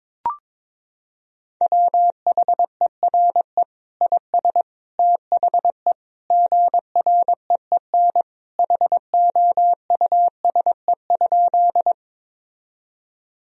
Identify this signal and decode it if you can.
Morse code, heard as a keyed tone, about 22 wpm, 700 Hz. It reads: WHERE IS THE GREEN HOUSE?
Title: Sentences from Top 1000 Words - Review Part 2 - 22wpm